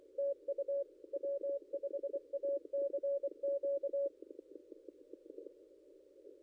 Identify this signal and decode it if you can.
Morse code, heard as a keyed tone, about 24 wpm, 560 Hz. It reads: TU W5ACQ
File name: contests / ss-cw-k5tr-op-n5rz -0800